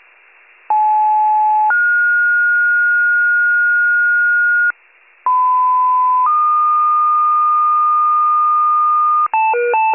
Two-tone page, heard as a one-second tone, then a three-second tone, about 860 then 1430 Hz, about 1010 then 1220 Hz, with noise. It is heard from 0.7 to 9.3 s.
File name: pager / 94 RESCUE OFC / wcalert